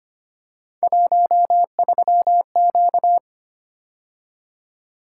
Morse code, heard as a keyed tone, about 25 wpm, 700 Hz. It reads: 13Q
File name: Three Letter-Number Combinations - 25wpm